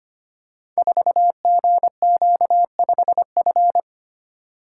Morse code, heard as a keyed tone, about 25 wpm, 700 Hz. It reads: 4GQ5F
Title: Code Groups - 5 Letter:Numbers - 25wpm